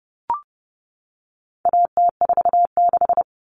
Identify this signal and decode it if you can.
Morse code, heard as a keyed tone, about 30 wpm, 700 Hz. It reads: AT46
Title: All Cut Numbers - 4-Digit 30wpm